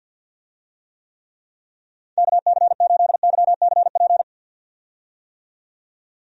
Morse code, heard as a keyed tone, about 51 wpm, 700 Hz. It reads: KC8YCP
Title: USA Call Signs - 6 Characters - 50wpm